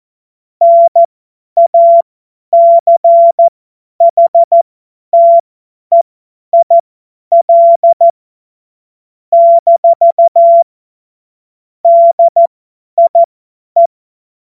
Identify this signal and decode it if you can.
Morse code, heard as a keyed tone, about 14 wpm, spaced as for 10 wpm, 680 Hz. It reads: NACHTEIL - DIE
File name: Morsekurs / Witze / Witz 02. August 2022 50 F70